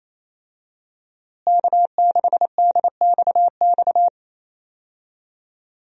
Morse code, heard as a keyed tone, about 28 wpm, 700 Hz. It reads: K6DXX
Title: USA Call Signs - 5 Characters - Repeated 3x - 28wpm